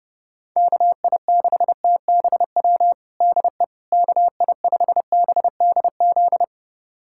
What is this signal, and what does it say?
Morse code, heard as a keyed tone, about 30 wpm, 700 Hz. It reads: KI6TBW DE KI5BDZ
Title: USA Callsign DE Callsign - 6 Characters - 30wpm